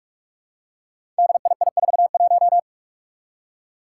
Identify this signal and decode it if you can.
Morse code, heard as a keyed tone, about 46 wpm, 700 Hz. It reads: DII41
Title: Five Challenging Letter-Number Combinations 45wpm